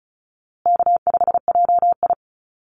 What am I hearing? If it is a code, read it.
Morse code, heard as a keyed tone, about 34 wpm, 700 Hz. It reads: K5JI
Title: USA Call Signs Repeated 3x 35wpm